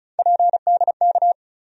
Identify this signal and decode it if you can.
Morse code, heard as a keyed tone, about 36 wpm, 700 Hz. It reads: PDK
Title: USA Call Sign Postfixes - 3 Letters 35wpm